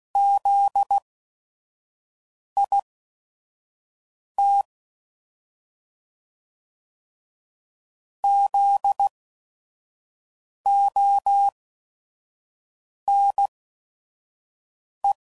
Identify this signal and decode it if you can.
Morse code, heard as a keyed tone, about 16 wpm, spaced as for 5 wpm, 800 Hz. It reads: ZIT ZONE
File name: Prefixes / By First Letter / Zs